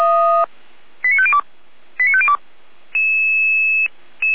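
Five-tone selective call (ZVEI), heard as quick runs of five tones, tones of about 70 ms. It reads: 89502 89502
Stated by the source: ZVEI five-tone call received via SDR, from Feuerwehr Orlamünde (fire brigade) server.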